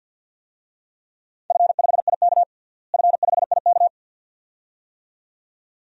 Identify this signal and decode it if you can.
Morse code, heard as a keyed tone, about 53 wpm, 700 Hz. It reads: F5IK F5IK
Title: DXCC-100 Call Signs - 4 Characters - Repeated 2x - 50wpm